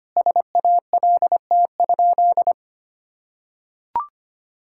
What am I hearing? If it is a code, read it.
Morse code, heard as a keyed tone, about 25 wpm, 700 Hz. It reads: SALT?
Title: Sentences from Top 1000 Words - Review Part 5 - 25wpm